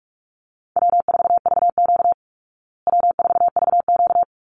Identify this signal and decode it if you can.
Morse code, heard as a keyed tone, about 44 wpm, 700 Hz. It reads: W4VQ W4VQ
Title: Call Signs Repeated 2x - 45wpm